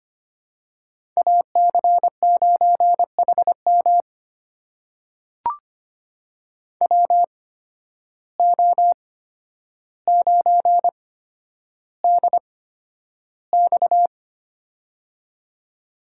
Morse code, heard as a keyed tone, about 25 wpm, spaced as for 11 wpm, 700 Hz. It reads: AC9HM WO9DX
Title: USA Call Signs - 5 Characters - 8x Character Spacing - 25wpm